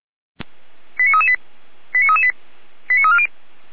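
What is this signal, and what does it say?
Five-tone selective call (ZVEI), heard as quick runs of five tones, tones of about 80 ms. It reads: 89338 89338 89340
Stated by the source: ZVEI five-tone selective call received by SDR.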